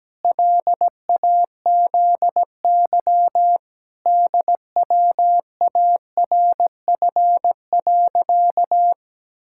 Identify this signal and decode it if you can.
Morse code, heard as a keyed tone, about 17 wpm, 700 Hz. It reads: LAZY DWARF.
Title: Pangram Challenge v2 - 17wpm